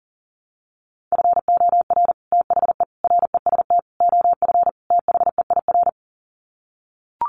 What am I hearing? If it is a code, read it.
Morse code, heard as a keyed tone, about 40 wpm, 700 Hz. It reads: FOR THE REST OF THEIR
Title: Sets of 5 Words Encore2 40wpm